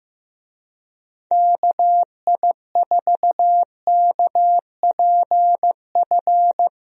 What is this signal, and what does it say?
Morse code, heard as a keyed tone, about 15 wpm, 700 Hz. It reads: KI4KPF
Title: USA Call Signs - 6 Characters - Repeated 3x - 15wpm